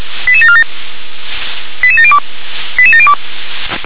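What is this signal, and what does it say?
Five-tone selective call (ZVEI), heard as quick runs of five tones, tones of about 70 ms. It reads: 88747 88702 88702
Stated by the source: ZVEI five-tone call received on an SDR.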